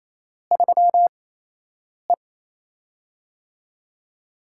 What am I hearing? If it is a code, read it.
Morse code, heard as a keyed tone, about 28 wpm, 700 Hz. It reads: 3 E
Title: USA Call Signs - 4 Characters - 8x Character Spacing - 28wpm